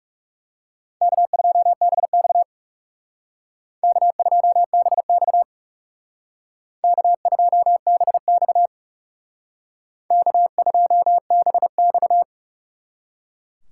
Morse code, heard as a keyed tone, about 39 wpm, 700 Hz. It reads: K2BX K2BX K2BX K2BX